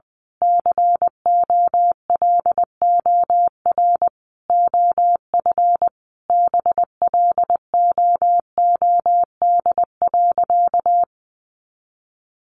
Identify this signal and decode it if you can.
Morse code, heard as a keyed tone, about 20 wpm, 700 Hz. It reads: COLOR OF BLOOD.
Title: A Sound Of Thunder 20wpm